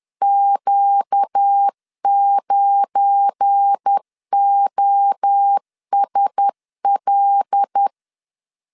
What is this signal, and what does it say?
Morse code, heard as a keyed tone, about 10 wpm, 800 Hz. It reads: Q9OSL
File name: CW / DET / 1 9 1500C S B 10wpm 4439